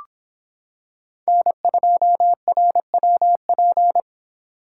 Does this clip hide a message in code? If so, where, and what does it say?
Yes: Morse code from 1.3 to 4.0 s, heard as a keyed tone, about 25 wpm, 700 Hz. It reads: N2RWP